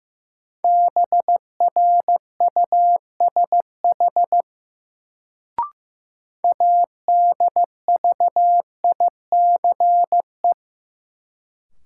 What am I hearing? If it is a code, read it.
Morse code, heard as a keyed tone, about 15 wpm, 700 Hz. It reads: BRUSH ADVICE
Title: Sentences from Top 1100 Words - New Word Drill Alt - 15wpm